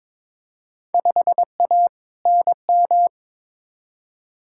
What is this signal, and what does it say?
Morse code, heard as a keyed tone, about 22 wpm, 700 Hz. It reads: 5A NM
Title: Field Day Exchanges - 22wpm